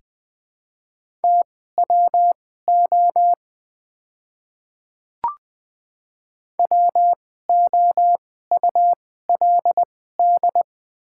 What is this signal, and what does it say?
Morse code, heard as a keyed tone, about 20 wpm, spaced as for 15 wpm, 700 Hz. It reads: TWO WOULD
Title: 2x Character Spacing - Top 100 Words - 20wpm